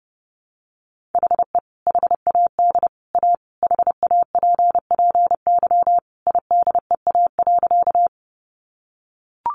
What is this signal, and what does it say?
Morse code, heard as a keyed tone, about 30 wpm, 700 Hz. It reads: HE HAD A HAPPY IDEA.
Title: Sentences from Top 1000 Words 30wpm